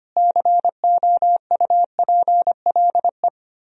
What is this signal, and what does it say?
Morse code, heard as a keyed tone, about 32 wpm, 700 Hz. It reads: COUPLE